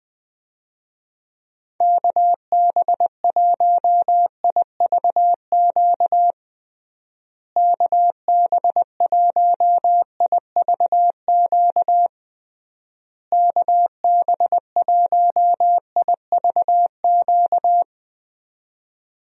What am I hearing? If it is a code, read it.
Morse code, heard as a keyed tone, about 20 wpm, 700 Hz. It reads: KB1IVQ KB1IVQ KB1IVQ
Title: USA Call Signs - 6 Characters - Repeated 3x - 20wpm